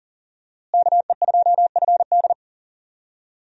Morse code, heard as a keyed tone, about 40 wpm, 700 Hz. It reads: KE2FD
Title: USA Call Signs - 5 Characters - Repeated 2x - 40wpm